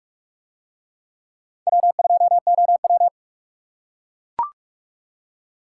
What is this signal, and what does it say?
Morse code, heard as a keyed tone, about 44 wpm, 700 Hz. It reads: W2OW